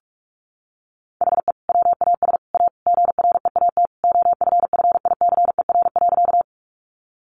Morse code, heard as a keyed tone, about 44 wpm, 700 Hz. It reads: HE WAS A GREAT OFFICER.
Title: Sentences from Top 1200 Words 45wpm